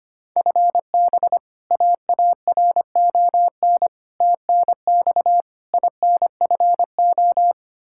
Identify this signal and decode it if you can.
Morse code, heard as a keyed tone, about 25 wpm, 700 Hz. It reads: FB AARON TNX INFO
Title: Practice QSO - Exchange 3 - More Info - 25wpm